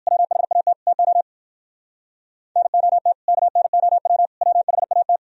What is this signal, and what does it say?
Morse code, heard as a keyed tone, about 56 wpm, 700 Hz. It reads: WHAT TO NOT KNOW WHAT